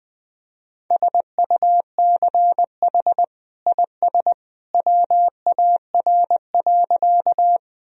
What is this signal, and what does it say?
Morse code, heard as a keyed tone, about 21 wpm, 700 Hz. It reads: SUCH IS WAR.
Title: Sentences from Top 1000 Words - Review Part 2 - 20wpm